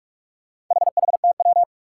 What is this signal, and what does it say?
Morse code, heard as a keyed tone, about 45 wpm, 700 Hz. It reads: HHTW
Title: Four Challenging Letter-Number Combinations 45wpm